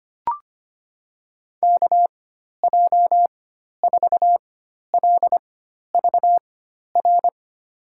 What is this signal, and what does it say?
Morse code, heard as a keyed tone, about 25 wpm, spaced as for 12 wpm, 700 Hz. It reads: KJ4LVR
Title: USA Call Signs - 6 Characters - 4x Character Spacing - 25wpm